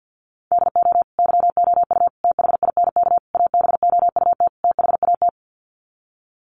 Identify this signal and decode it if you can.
Morse code, heard as a keyed tone, about 48 wpm, 700 Hz. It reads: DO YOU THINK ABOUT THAT
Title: Sets of 5 Words Encore2 50wpm